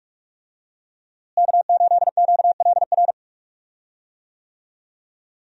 Morse code, heard as a keyed tone, about 46 wpm, 700 Hz. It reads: K8QLR
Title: USA Call Signs - 5 Characters - Repeated 3x - 45wpm